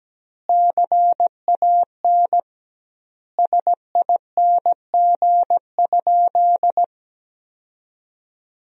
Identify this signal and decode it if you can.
Morse code, heard as a keyed tone, about 17 wpm, 700 Hz. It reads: CAN SING?
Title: 2x Word-Spacing - Sentences from Top 400 Words - 17wpm